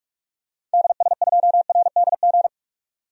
Morse code, heard as a keyed tone, about 46 wpm, 700 Hz. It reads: DS2RDG
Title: DXCC-100 Call Signs - 6 Characters - 45wpm